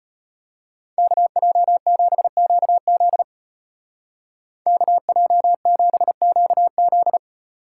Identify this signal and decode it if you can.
Morse code, heard as a keyed tone, about 30 wpm, 700 Hz. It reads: KJ7QZ KJ7QZ
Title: Speed Racing - USA 2x2 Call Signs 25wpm